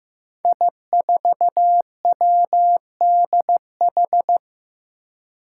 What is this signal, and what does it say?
Morse code, heard as a keyed tone, about 15 wpm, 700 Hz. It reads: I4WDH
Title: Five Challenging Letter-Number Combinations 15wpm